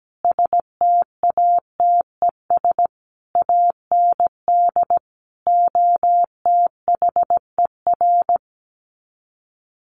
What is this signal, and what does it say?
Morse code, heard as a keyed tone, about 17 wpm, 700 Hz. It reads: STATES AND OTHER
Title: Sets of 5 Words Encore 17wpm